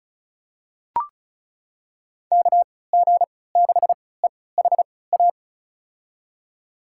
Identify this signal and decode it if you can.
Morse code, heard as a keyed tone, about 34 wpm, spaced as for 20 wpm, 700 Hz. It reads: KG6EHA
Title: USA Call Signs - 6 Characters - 3x Character Spacing - 35wpm